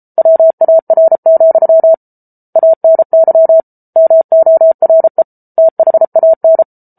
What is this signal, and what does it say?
Morse code, heard as a keyed tone, about 34 wpm, 650 Hz. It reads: WAR, ANY MORE THAN